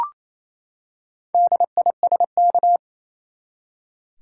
Morse code, heard as a keyed tone, about 28 wpm, 700 Hz. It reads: DISK
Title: Four Letter Words - 28wpm